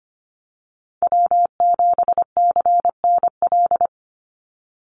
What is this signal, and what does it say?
Morse code, heard as a keyed tone, about 25 wpm, 700 Hz. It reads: W7CNL
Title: NA SOTA Chaser Callsigns 25wpm